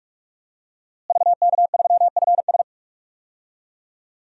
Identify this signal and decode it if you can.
Morse code, heard as a keyed tone, about 46 wpm, 700 Hz. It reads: VK3FS
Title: USA Call Signs - 5 Characters - Repeated 2x - 45wpm